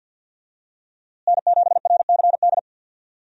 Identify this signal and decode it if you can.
Morse code, heard as a keyed tone, about 48 wpm, 700 Hz. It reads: N7RCD
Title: USA Call Signs - 5 Characters - Repeated 3x - 50wpm